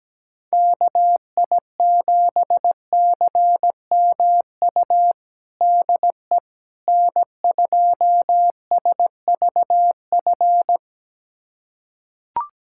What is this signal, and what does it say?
Morse code, heard as a keyed tone, about 17 wpm, 700 Hz. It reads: KI7CMU DE N2SVF
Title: USA Callsign DE Callsign 17wpm